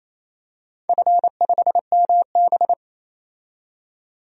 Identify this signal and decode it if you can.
Morse code, heard as a keyed tone, about 27 wpm, 700 Hz. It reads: F5MB
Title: DXCC-100 Call Signs - 4 Characters - 28wpm